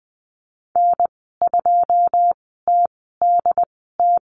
Morse code, heard as a keyed tone, about 20 wpm, spaced as for 15 wpm, 700 Hz. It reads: N2TDT
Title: 128 Lesson 21g - Number 6 - Call Signs 20wpm